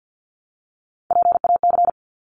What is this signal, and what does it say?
Morse code, heard as a keyed tone, about 50 wpm, 700 Hz. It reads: PAC
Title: ARRL Sections 50wpm